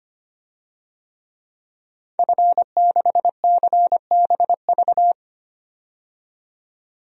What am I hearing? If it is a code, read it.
Morse code, heard as a keyed tone, about 25 wpm, 700 Hz. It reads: F6CBV
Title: DXCC-100 Call Signs - 5 Characters - 25wpm